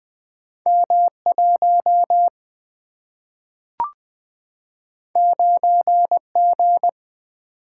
Morse code, heard as a keyed tone, about 20 wpm, 700 Hz. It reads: M1 9G
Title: Single-Letter Single-Number 20wpm